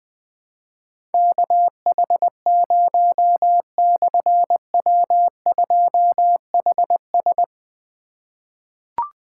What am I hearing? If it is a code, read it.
Morse code, heard as a keyed tone, about 20 wpm, 700 Hz. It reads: KH0/W2HS
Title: Difficult Call Signs 20wpm